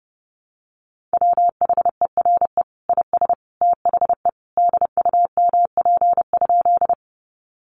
Morse code, heard as a keyed tone, about 30 wpm, 700 Hz. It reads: WHERE IS THE DUMP?
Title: Sentences from Top 1300 Words 30wpm